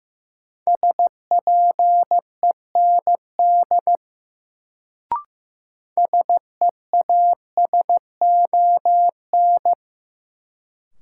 Morse code, heard as a keyed tone, about 15 wpm, 700 Hz. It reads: SPEND SEASON
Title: Sets of 5 Words - New Word Drill - 15wpm